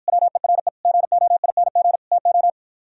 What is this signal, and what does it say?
Morse code, heard as a keyed tone, about 52 wpm, 700 Hz. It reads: WERE GOING TO